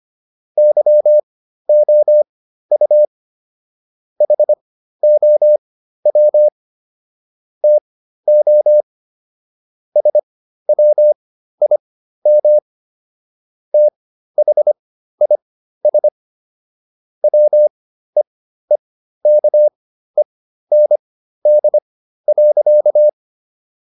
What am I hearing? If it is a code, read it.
Morse code, heard as a keyed tone, about 25 wpm, spaced as for 13 wpm, 600 Hz. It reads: YOU HOW TO SWIM THIS WEEKEND.